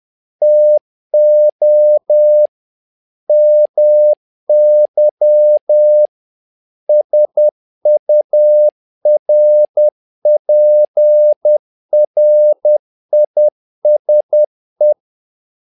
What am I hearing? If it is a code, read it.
Morse code, heard as a keyed tone, about 10 wpm, 600 Hz. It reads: TO MY SURPRISE
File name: learning center / tower trouble / tower trouble 10wpm